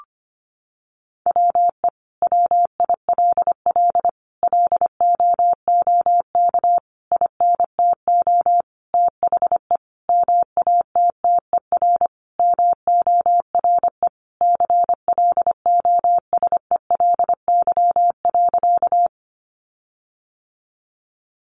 Morse code, heard as a keyed tone, about 25 wpm, 700 Hz. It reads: WE WILL LOOK INTO THE MATTER MORE CLOSELY.
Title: Sentences from Top 1400 Words 25wpm